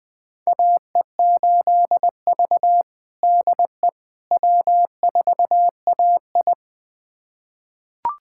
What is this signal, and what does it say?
Morse code, heard as a keyed tone, about 21 wpm, 700 Hz. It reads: AE8V DE W4AI
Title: USA Callsign DE Callsign - 4 Characters - 20wpm